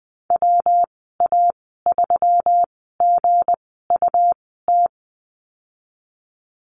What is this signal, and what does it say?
Morse code, heard as a keyed tone, about 20 wpm, spaced as for 15 wpm, 700 Hz. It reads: WA3GUT
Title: 219 Lesson 35g - Number 0 - Call Signs 20wpm